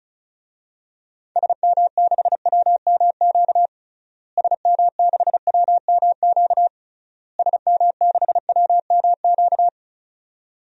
Morse code, heard as a keyed tone, about 34 wpm, 700 Hz. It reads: SM6WMQ SM6WMQ SM6WMQ
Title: DXCC-100 Call Signs - 6 Characters - Repeated 3x - 35wpm